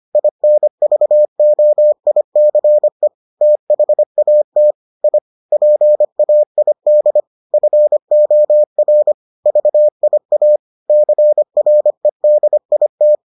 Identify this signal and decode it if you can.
Morse code, heard as a keyed tone, about 25 wpm, 600 Hz. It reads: INVOICE THAT I PAID FOR VIA CREDIT